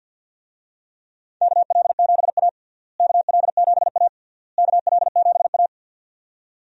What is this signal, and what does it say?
Morse code, heard as a keyed tone, about 52 wpm, 700 Hz. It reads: KL7A KL7A KL7A
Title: DXCC-100 Call Signs - 4 Characters - Repeated 3x - 50wpm